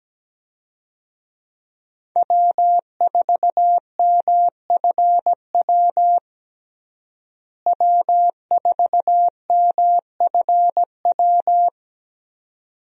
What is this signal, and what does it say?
Morse code, heard as a keyed tone, about 17 wpm, 700 Hz. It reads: W4MFW W4MFW
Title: USA Call Signs - 5 Characters - Repeated 2x - 17wpm